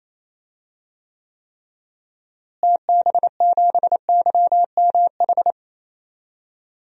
Morse code, heard as a keyed tone, about 28 wpm, 700 Hz. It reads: TB7YMH
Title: DXCC-100 Call Signs - 6 Characters - Repeated 3x - 28wpm